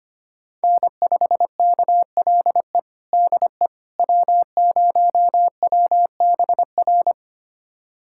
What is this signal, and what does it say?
Morse code, heard as a keyed tone, about 25 wpm, 700 Hz. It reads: N5KLE DE W0WBR
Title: USA Callsign DE Callsign - 5 Characters - 25wpm